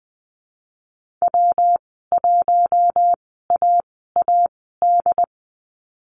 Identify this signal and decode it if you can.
Morse code, heard as a keyed tone, about 20 wpm, spaced as for 15 wpm, 700 Hz. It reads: W1AAD
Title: 122 Lesson 20g - Number 3 - Call Signs 20wpm